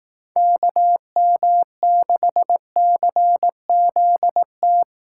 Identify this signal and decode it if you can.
Morse code, heard as a keyed tone, about 18 wpm, 700 Hz. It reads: KM6CZT
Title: Speed Racing - USA 2x3 Call Signs 15wpm